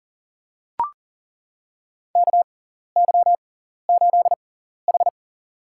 Morse code, heard as a keyed tone, about 40 wpm, spaced as for 14 wpm, 700 Hz. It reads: KY8H
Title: USA Call Signs - 4 Characters - 6x Character Spacing - 40wpm